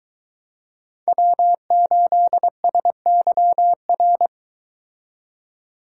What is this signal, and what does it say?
Morse code, heard as a keyed tone, about 23 wpm, 700 Hz. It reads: W8SYR